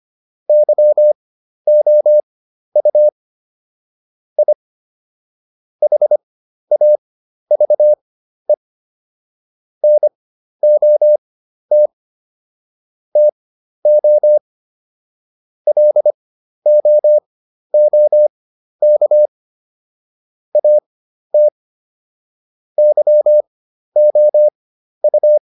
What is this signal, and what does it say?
Morse code, heard as a keyed tone, about 25 wpm, spaced as for 12 wpm, 600 Hz. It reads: YOU I HAVE NOT TO LOOK AT YOU